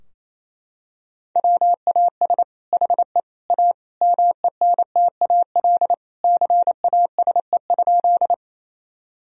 Morse code, heard as a keyed tone, about 28 wpm, 700 Hz. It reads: WAS HE A MENTAL CASE?
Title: Sentences from Top 1000 Words - Review Part 4 - 28wpm